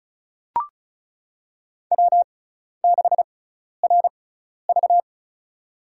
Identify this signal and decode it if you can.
Morse code, heard as a keyed tone, about 34 wpm, spaced as for 12 wpm, 700 Hz. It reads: W6RV
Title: USA Call Signs - 4 Characters - 6x Character Spacing - 35wpm